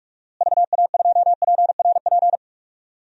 Morse code, heard as a keyed tone, about 46 wpm, 700 Hz. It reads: VA2PRP
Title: DXCC-100 Call Signs - 6 Characters - Repeated 2x - 45wpm